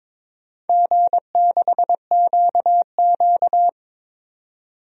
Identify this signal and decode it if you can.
Morse code, heard as a keyed tone, about 22 wpm, 700 Hz. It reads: G6QQ
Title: DXCC-100 Call Signs - 4 Characters - 22wpm